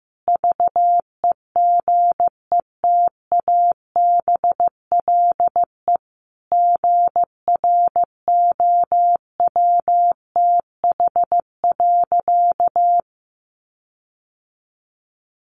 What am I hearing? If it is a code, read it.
Morse code, heard as a keyed tone, about 15 wpm, 700 Hz. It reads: VEGETABLE GROWTH.
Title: Sentences from Top 1300 Words 15wpm